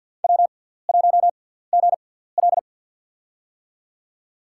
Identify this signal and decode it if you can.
Morse code, heard as a keyed tone, about 52 wpm, spaced as for 17 wpm, 700 Hz. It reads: W1GL
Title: USA Call Signs - 4 Characters - 6x Character Spacing - 50wpm